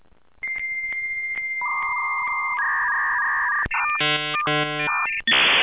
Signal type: dial-up modem handshake